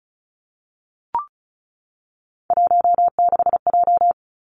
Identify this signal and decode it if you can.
Morse code, heard as a keyed tone, about 34 wpm, 700 Hz. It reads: 16J
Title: Three Challenging Letter-Number Combinations 35wpm